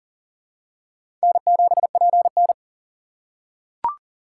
Morse code, heard as a keyed tone, about 40 wpm, 700 Hz. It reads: N7PN
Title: USA Call Signs - 6x Character Spacing - 40wpm